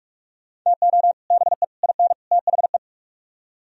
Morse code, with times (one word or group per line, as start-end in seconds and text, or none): 0.7-1.1 TO
1.3-1.6 BE
1.8-2.1 IN
2.3-2.8 THE